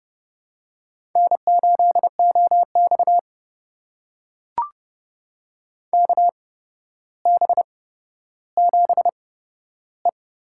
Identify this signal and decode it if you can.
Morse code, heard as a keyed tone, about 30 wpm, spaced as for 13 wpm, 700 Hz. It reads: N8OX KB7E
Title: USA Call Signs - 8x Character Spacing - 30wpm